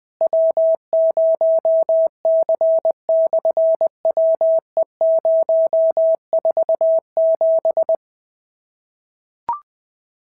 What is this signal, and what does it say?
Morse code, heard as a keyed tone, about 20 wpm, 650 Hz. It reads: W0C/WE047
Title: SOTA QSO Practice 20wpm